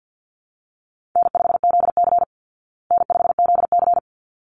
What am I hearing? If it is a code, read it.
Morse code, heard as a keyed tone, about 49 wpm, 700 Hz. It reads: N5ZC N5ZC
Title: Easy Call Signs Repeated 2x - 50wpm